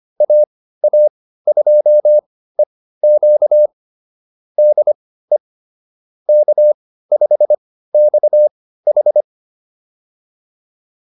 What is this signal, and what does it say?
Morse code, heard as a keyed tone, about 25 wpm, spaced as for 15 wpm, 600 Hz. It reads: AA2EQ DE K5XH